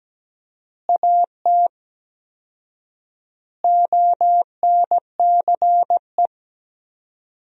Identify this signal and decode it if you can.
Morse code, heard as a keyed tone, about 17 wpm, 700 Hz. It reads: AT ONCE
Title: 4x Word-Spacing - Sentences from Top 300 Words - 17wpm